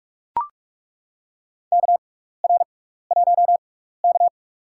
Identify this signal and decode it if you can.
Morse code, heard as a keyed tone, about 41 wpm, spaced as for 15 wpm, 700 Hz. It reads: KR1K
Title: USA Call Signs - 4 Characters - 6x Character Spacing - 45wpm